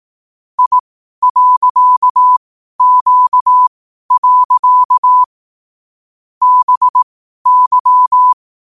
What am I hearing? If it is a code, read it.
Morse code, heard as a keyed tone, about 18 wpm, spaced as for 12 wpm, 1000 Hz. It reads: I.Q. BY
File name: qotdfiles / 12wpm